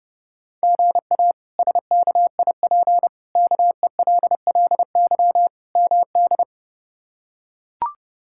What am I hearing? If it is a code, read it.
Morse code, heard as a keyed tone, about 29 wpm, 700 Hz. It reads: GA SKIP KELLY MD
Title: K1USN SST QSO Practice - 30wpm